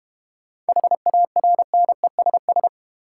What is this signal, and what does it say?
Morse code, heard as a keyed tone, about 31 wpm, 690 Hz. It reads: HARNESS